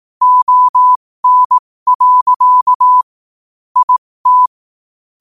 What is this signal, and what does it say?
Morse code, heard as a keyed tone, about 18 wpm, spaced as for 15 wpm, 1000 Hz. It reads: ON. IT
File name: qotdfiles / 15wpm